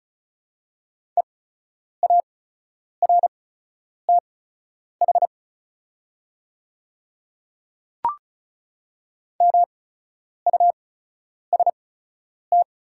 Morse code, heard as a keyed tone, about 36 wpm, spaced as for 10 wpm, 700 Hz. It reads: EARTH MUST
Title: 8x Character Spacing - Top 500 Words - 35wpm